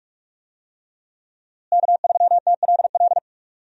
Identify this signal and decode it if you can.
Morse code, heard as a keyed tone, about 41 wpm, 700 Hz. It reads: K3TLL
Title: USA Call Signs - 5 Characters - Repeated 3x - 45wpm